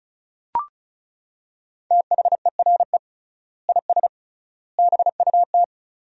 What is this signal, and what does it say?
Morse code, heard as a keyed tone, about 36 wpm, 700 Hz. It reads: THERE IS BUT
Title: 3x Word-Spacing - Sentences from Top 100 Words - 35wpm